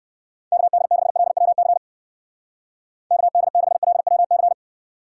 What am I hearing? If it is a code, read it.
Morse code, heard as a keyed tone, about 59 wpm, 700 Hz. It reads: KD6LWC KD6LWC